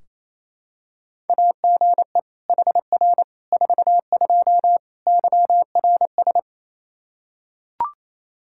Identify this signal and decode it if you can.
Morse code, heard as a keyed tone, about 27 wpm, 700 Hz. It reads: AGE HR 42 YRS
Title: Practice QSO - Exchange 3 - More Info - 28wpm